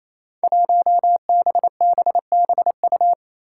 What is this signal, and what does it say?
Morse code, heard as a keyed tone, about 28 wpm, 700 Hz. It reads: 1BBBU
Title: Five Challenging Letter-Number Combinations 28wpm